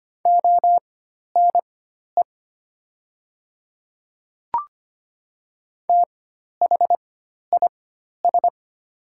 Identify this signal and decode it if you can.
Morse code, heard as a keyed tone, about 25 wpm, spaced as for 12 wpm, 700 Hz. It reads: ONE THIS
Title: 4x Character Spacing - Top 300 Words - 25wpm